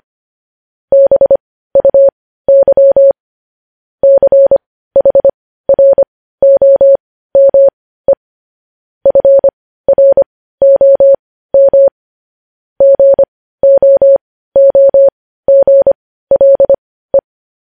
Morse code, heard as a keyed tone, about 25 wpm, spaced as for 15 wpm, 550 Hz. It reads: BUY CHROME FROM GOOGLE
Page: News Headlines in Morse Code at 15 WPM